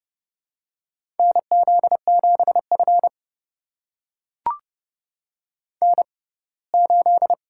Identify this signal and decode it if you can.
Morse code, heard as a keyed tone, about 32 wpm, spaced as for 21 wpm, 700 Hz. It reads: NZ7F N8
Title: USA Call Signs - 4 Characters - 6x Character Spacing - 30wpm